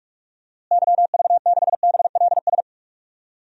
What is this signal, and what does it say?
Morse code, heard as a keyed tone, about 46 wpm, 700 Hz. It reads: YV6BLS
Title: DXCC-100 Call Signs - 6 Characters - Repeated 2x - 45wpm